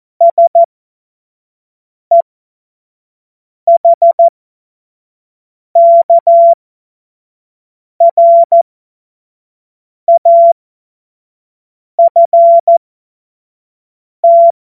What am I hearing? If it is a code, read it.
Morse code, heard as a keyed tone, about 14 wpm, spaced as for 5 wpm, 680 Hz. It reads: SEHKRAFT
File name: Morsekurs / Witze / Witz 26 März 2024 25 F70